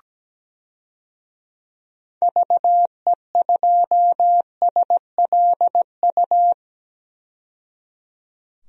Morse code, heard as a keyed tone, about 17 wpm, 700 Hz. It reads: VE2SLU
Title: DXCC-100 Call Signs - 6 Characters - 17wpm